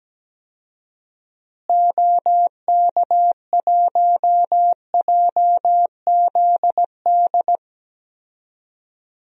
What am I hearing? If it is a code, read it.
Morse code, heard as a keyed tone, about 17 wpm, 700 Hz. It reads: OK1JZD
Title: DXCC-100 Call Signs - 6 Characters - Repeated 2x - 17wpm